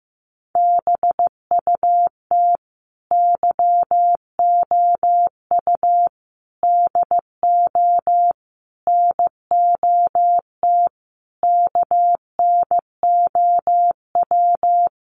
Morse code, heard as a keyed tone, about 15 wpm, 700 Hz. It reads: BUT YOU DO NOT KNOW
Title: Sentences from Top 600 Words 15wpm